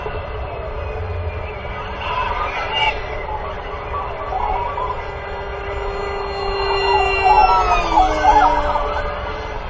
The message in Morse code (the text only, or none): none